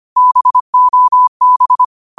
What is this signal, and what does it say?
Morse code, heard as a keyed tone, about 25 wpm, 1000 Hz. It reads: BOB